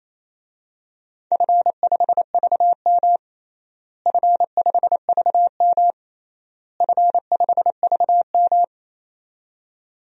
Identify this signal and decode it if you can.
Morse code, heard as a keyed tone, about 27 wpm, 700 Hz. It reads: F5VM F5VM F5VM
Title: DXCC-100 Call Signs - 4 Characters - Repeated 3x - 28wpm